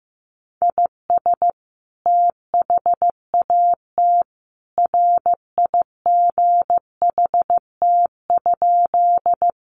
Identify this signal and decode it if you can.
Morse code, heard as a keyed tone, about 15 wpm, 700 Hz. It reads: IS THAT RIGHT?
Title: Sentences from Top 200 Words 15wpm